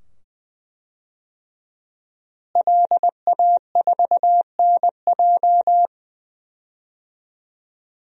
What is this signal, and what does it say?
Morse code, heard as a keyed tone, about 20 wpm, 700 Hz. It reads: LA4NJ
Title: DXCC-100 Call Signs - 5 Characters - Repeated 2x - 20wpm